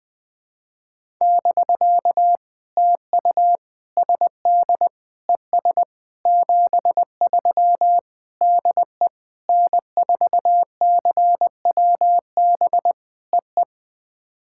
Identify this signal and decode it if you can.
Morse code, heard as a keyed tone, about 20 wpm, 700 Hz. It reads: <BK> TU SD ES 73 DE N4CWB EE
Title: POTA QSO Practice - 20wpm